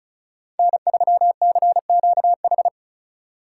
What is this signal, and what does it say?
Morse code, heard as a keyed tone, about 36 wpm, 700 Hz. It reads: N3CQH
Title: USA Call Signs - 5 Characters - 3x Character Spacing - 35wpm